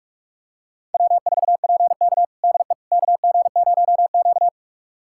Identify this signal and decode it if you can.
Morse code, heard as a keyed tone, about 42 wpm, 700 Hz. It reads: W4PK DE KG0Q
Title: USA Callsign DE Callsign - 4 Characters - 45wpm